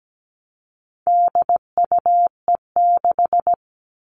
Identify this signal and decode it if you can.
Morse code, heard as a keyed tone, about 17 wpm, 700 Hz. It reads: DUE6
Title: All Cut Numbers - 4-Digit 17wpm